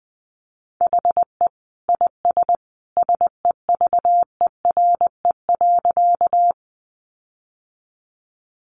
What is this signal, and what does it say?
Morse code, heard as a keyed tone, about 20 wpm, 700 Hz. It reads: HE IS SEVERE.
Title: Sentences from Top 1400 Words 20wpm